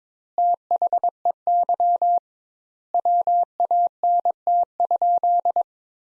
Morse code, heard as a keyed tone, about 21 wpm, 700 Hz. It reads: THEY WANT?
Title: Sentences Top 200 Words - 22wpm